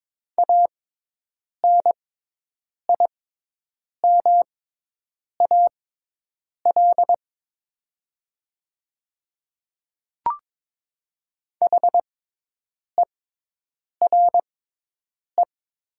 Morse code, heard as a keyed tone, about 22 wpm, spaced as for 8 wpm, 700 Hz. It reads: ANIMAL HERE
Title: 6x Character Spacing - Top 200 Words - 22wpm